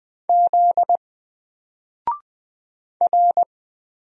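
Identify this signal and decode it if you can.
Morse code, heard as a keyed tone, about 20 wpm, 700 Hz. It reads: Z R
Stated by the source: Single letter and number Morse drill at 20 wpm, from N4LCD.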